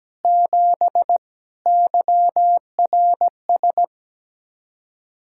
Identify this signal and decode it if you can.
Morse code, heard as a keyed tone, about 17 wpm, 700 Hz. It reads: 7 YRS
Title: K1USN SST QSO Practice 17wpm